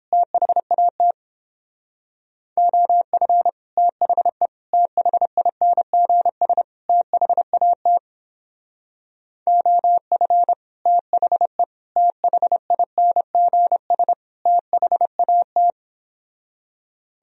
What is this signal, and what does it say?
Morse code, heard as a keyed tone, about 31 wpm, 700 Hz. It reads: THAT OF THE THINGS THAT OF THE THINGS THAT